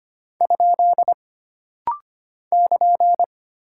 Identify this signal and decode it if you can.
Morse code, heard as a keyed tone, about 25 wpm, 700 Hz. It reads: ? <KN>
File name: Single Punctuation-Prosign - Rapid-Fire - 25wpm